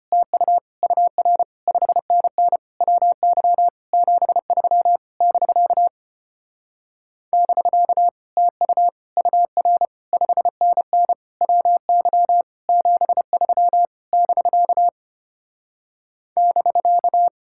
Morse code, heard as a keyed tone, about 33 wpm, 700 Hz. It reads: TU UR 5NN WY 73 <BK> <BK> TU UR 5NN WY 73 <BK> <BK>